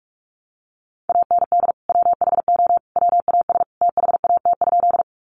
Morse code, heard as a keyed tone, about 44 wpm, 700 Hz. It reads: AND WHO WAS THAT?
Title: Sentences from Top 100 Words 45wpm